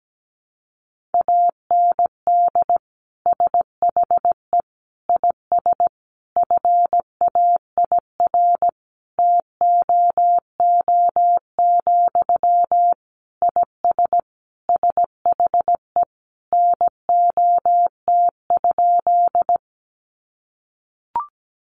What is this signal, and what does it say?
Morse code, heard as a keyed tone, about 17 wpm, 700 Hz. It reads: AND SHE IS FAIR TOO, IS SHE NOT?
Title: Sentences from Top 1300 Words 17wpm